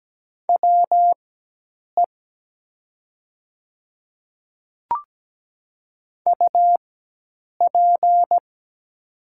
Morse code, heard as a keyed tone, about 17 wpm, spaced as for 8 wpm, 700 Hz. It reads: WE UP
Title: 4x Character Spacing - Top 100 Words - 17wpm